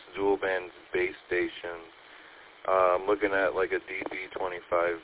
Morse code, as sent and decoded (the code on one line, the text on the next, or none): none